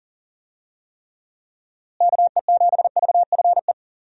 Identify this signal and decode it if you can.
Morse code, heard as a keyed tone, about 40 wpm, 700 Hz. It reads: KE7VFE
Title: USA Call Signs - 6 Characters - Repeated 3x - 40wpm